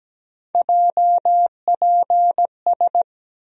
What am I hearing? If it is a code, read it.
Morse code, heard as a keyed tone, about 17 wpm, 700 Hz. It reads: JPS
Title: USA Call Sign Postfixes - 3 Letters 17wpm